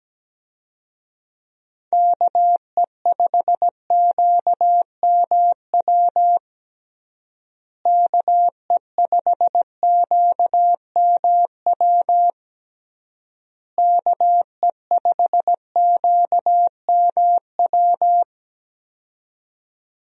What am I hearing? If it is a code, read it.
Morse code, heard as a keyed tone, about 17 wpm, 700 Hz. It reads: KE5QMW KE5QMW KE5QMW
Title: USA Call Signs - 6 Characters - Repeated 3x - 17wpm